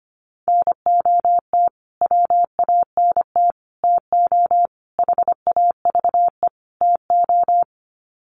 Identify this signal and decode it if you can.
Morse code, heard as a keyed tone, about 25 wpm, 700 Hz. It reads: NOT WANT TO HAVE TO
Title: Sets of 5 Words Encore2 25wpm